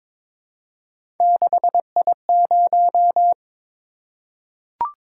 Morse code, heard as a keyed tone, about 23 wpm, 700 Hz. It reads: 6I0
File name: practice / 22wpm / Three Letter-Number Combinations - 22wpm